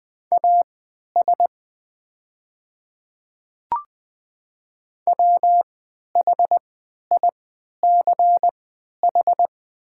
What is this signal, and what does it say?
Morse code, heard as a keyed tone, about 20 wpm, spaced as for 11 wpm, 700 Hz. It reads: AS WHICH
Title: 3x Character Spacing - Top 100 Words - 20wpm